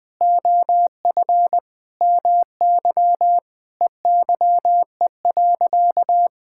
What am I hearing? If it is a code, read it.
Morse code, heard as a keyed tone, about 20 wpm, 700 Hz. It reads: OF MY EYE.
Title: Sentences with 7-Word Phrases - Spoken Beginning - 20wpm